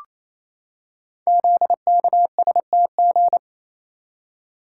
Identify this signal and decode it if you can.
Morse code, heard as a keyed tone, about 28 wpm, 700 Hz. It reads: ZKSTG
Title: Code Groups - 5 Letters - 28wpm